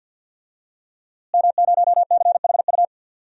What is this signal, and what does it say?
Morse code, heard as a keyed tone, about 47 wpm, 700 Hz. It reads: M0CHU
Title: DXCC-100 Call Signs - 5 Characters - Repeated 2x - 50wpm